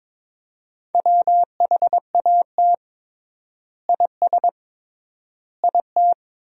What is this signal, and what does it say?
Morse code, heard as a keyed tone, about 22 wpm, 700 Hz. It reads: WHAT IS IT
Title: 3x Word-Spacing - Sentences from Top 100 Words - 22wpm